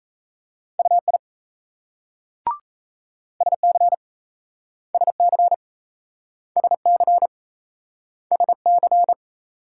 Morse code, heard as a keyed tone, about 37 wpm, 700 Hz. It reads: UI SC SC SC SC